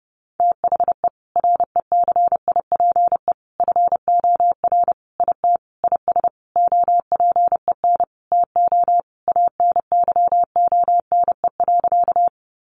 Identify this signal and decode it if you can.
Morse code, heard as a keyed tone, about 30 wpm, 700 Hz. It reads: THE RECIPE FOR IT IS OPEN TO ANYONE.
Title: Sentences from Top 1200 Words 30wpm